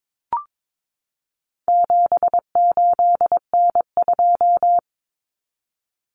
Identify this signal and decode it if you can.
Morse code, heard as a keyed tone, about 22 wpm, 700 Hz. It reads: 78N2
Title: Common Cut Numbers - 4-Digit 22wpm